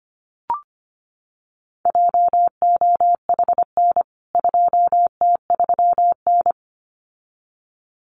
Morse code, heard as a keyed tone, about 25 wpm, 700 Hz. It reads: JOHN 2T3N
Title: CWT Exchange 25wpm